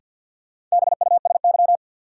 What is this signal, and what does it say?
Morse code, heard as a keyed tone, about 46 wpm, 700 Hz. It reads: BUSY
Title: Sentences from Top 1000 Words - Review Part 2 - New Word Drill - 50wpm